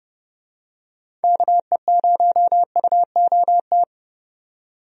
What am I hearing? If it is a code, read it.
Morse code, heard as a keyed tone, about 30 wpm, 700 Hz. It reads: KE0UOT
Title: USA Call Signs - 6 Characters - 30wpm